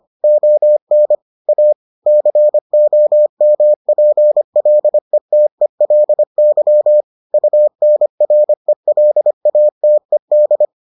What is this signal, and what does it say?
Morse code, heard as a keyed tone, about 25 wpm, 600 Hz. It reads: ON A COMPLETELY UNRELATED